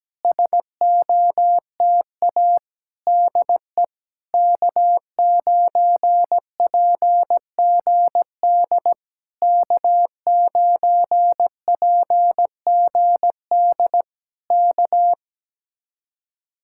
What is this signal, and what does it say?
Morse code, heard as a keyed tone, about 17 wpm, 700 Hz. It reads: SOTA DE K9PGD K9PGD K
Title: Speed-Racing Common QSO Phrases - 17wpm